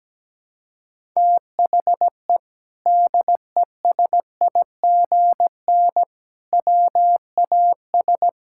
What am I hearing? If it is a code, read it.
Morse code, heard as a keyed tone, about 17 wpm, 700 Hz. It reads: THE DESIGN WAS
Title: Sentences from Top 1000 Words - Review Part 5 - 17wpm